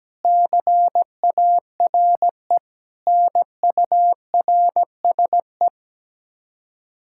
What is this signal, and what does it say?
Morse code, heard as a keyed tone, about 17 wpm, 700 Hz. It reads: CARE NURSE
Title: Practice QSO - Exchange 3 - More Info - 17wpm